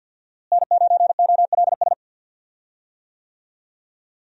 Morse code, heard as a keyed tone, about 53 wpm, 700 Hz. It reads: N9OLS
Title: USA Call Signs - 5 Characters - Repeated 2x - 50wpm